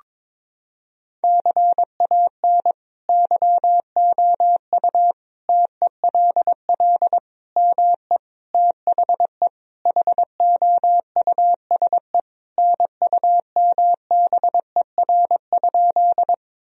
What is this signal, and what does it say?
Morse code, heard as a keyed tone, about 22 wpm, 700 Hz. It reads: CAN YOU TELL ME THE HOUSE NUMBER?
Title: Sentences from Top 1000 Words - Review Part 1 - 22wpm